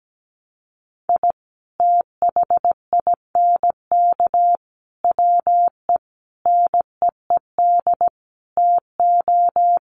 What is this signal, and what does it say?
Morse code, heard as a keyed tone, about 17 wpm, 700 Hz. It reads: I THINK WE NEED TO
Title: Sets of 5 Words 17wpm